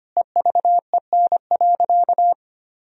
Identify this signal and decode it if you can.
Morse code, heard as a keyed tone, about 25 wpm, 700 Hz. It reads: EVEN.
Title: Sentences from Top 1000 Words - Review Part 1 - 25wpm